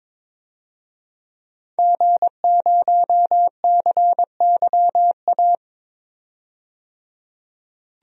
Morse code, heard as a keyed tone, about 22 wpm, 700 Hz. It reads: G0CYA
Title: DXCC-100 Call Signs - 5 Characters - 22wpm